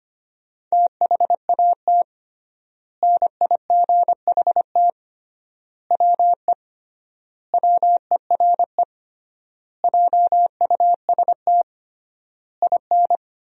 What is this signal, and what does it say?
Morse code, heard as a keyed tone, about 25 wpm, 700 Hz. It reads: THAT NIGHT WE WERE JUST IN
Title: 3x Word-Spacing - Sentences from Top 300 Words - 25wpm